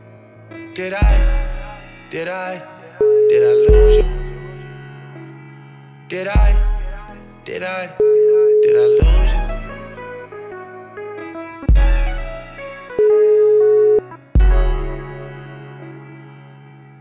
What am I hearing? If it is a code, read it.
Ringback tone (Europe, 425 Hz), with speech.